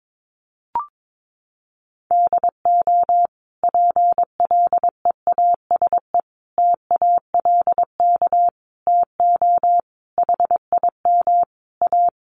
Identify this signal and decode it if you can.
Morse code, heard as a keyed tone, about 22 wpm, 700 Hz. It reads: DO PLEASE TALK TO HIM A
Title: Sentences from Top 1500 Words 22wpm